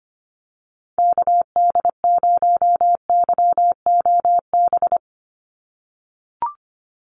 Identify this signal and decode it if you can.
Morse code, heard as a keyed tone, about 25 wpm, 700 Hz. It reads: KD0YOB
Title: NA SOTA Activator Callsigns 25wpm